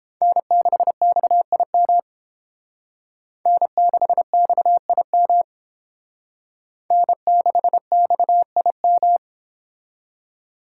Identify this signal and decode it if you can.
Morse code, heard as a keyed tone, about 28 wpm, 700 Hz. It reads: N6XIM N6XIM N6XIM